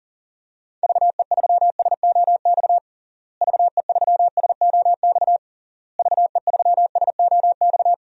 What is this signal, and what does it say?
Morse code, heard as a keyed tone, about 37 wpm, 700 Hz. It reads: VE3SOX VE3SOX VE3SOX
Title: DXCC-100 Call Signs - 6 Characters - Repeated 3x - 40wpm